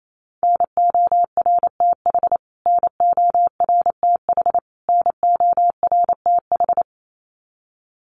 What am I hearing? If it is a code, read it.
Morse code, heard as a keyed tone, about 28 wpm, 700 Hz. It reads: NORTH NORTH NORTH
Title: Top 300 Words Repeated 28wpm